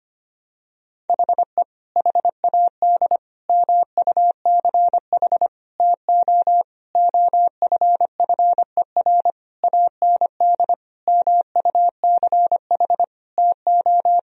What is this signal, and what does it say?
Morse code, heard as a keyed tone, about 25 wpm, 700 Hz. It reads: HE HAD MUCH TO OFFER AND MUCH TO
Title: Sentences from Top 1000 Words - Review Part 4 - 25wpm